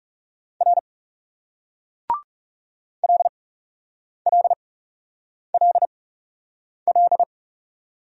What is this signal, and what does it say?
Morse code, heard as a keyed tone, about 42 wpm, spaced as for 8 wpm, 700 Hz. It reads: R LLLL